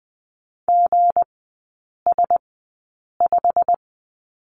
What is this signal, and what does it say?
Morse code, heard as a keyed tone, about 20 wpm, spaced as for 8 wpm, 690 Hz. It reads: GS5
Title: 159 Lesson 26e - Letter g - Three Characters 20wpm